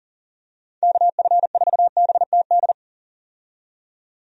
Morse code, heard as a keyed tone, about 40 wpm, 700 Hz. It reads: KF4BTD
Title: USA Call Signs - 6 Characters - 40wpm